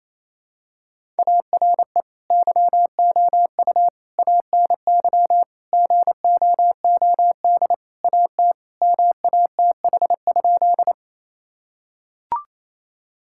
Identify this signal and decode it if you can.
Morse code, heard as a keyed tone, about 28 wpm, 700 Hz. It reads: ARE YOU ANY GOOD AT MATH?
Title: Sentences from Top 1000 Words - Review Part 6 - 28wpm